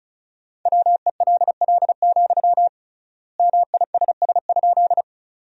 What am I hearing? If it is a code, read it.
Morse code, heard as a keyed tone, about 36 wpm, 700 Hz. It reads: WELL, MISS?
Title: 3x Word-Spacing - Sentences from Top 500 Words - 35wpm